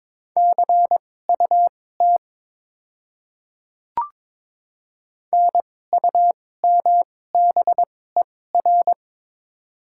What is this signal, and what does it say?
Morse code, heard as a keyed tone, about 22 wpm, spaced as for 16 wpm, 700 Hz. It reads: CUT NUMBER
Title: 2x Character Spacing - Top 500 Words - 22wpm